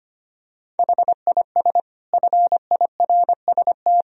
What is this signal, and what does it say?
Morse code, heard as a keyed tone, about 25 wpm, 700 Hz. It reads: HIS FIRST
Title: Sentences from Top 500 Words and Abbreviations 25wpm